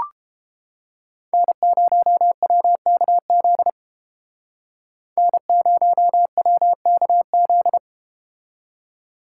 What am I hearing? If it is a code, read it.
Morse code, heard as a keyed tone, about 33 wpm, 700 Hz. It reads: N0WKZ N0WKZ